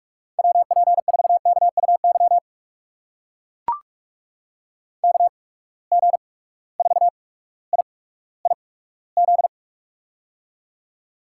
Morse code, heard as a keyed tone, about 43 wpm, spaced as for 19 wpm, 700 Hz. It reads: WP4KUY KG4IIZ
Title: USA Call Signs - 6 Characters - 8x Character Spacing - 45wpm